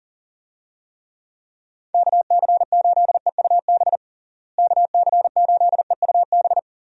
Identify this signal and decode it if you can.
Morse code, heard as a keyed tone, about 40 wpm, 700 Hz. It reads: KC8EUB KC8EUB
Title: USA Call Signs - 6 Characters - Repeated 2x - 40wpm